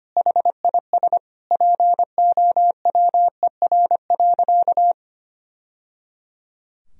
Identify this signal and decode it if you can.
Morse code, heard as a keyed tone, about 25 wpm, 700 Hz. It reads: HIS POWER.
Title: Sentences from Top 1000 Words - Review Part 2 - 25wpm